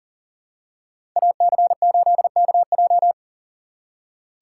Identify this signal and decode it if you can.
Morse code, heard as a keyed tone, about 40 wpm, 700 Hz. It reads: AC8KJ
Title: USA Call Signs - 2x Character Spacing - 40wpm